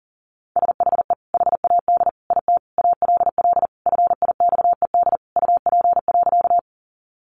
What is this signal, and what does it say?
Morse code, heard as a keyed tone, about 40 wpm, 700 Hz. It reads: SHE HAD IT ALL FIXED UP.
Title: Sentences from Top 1100 Words 40wpm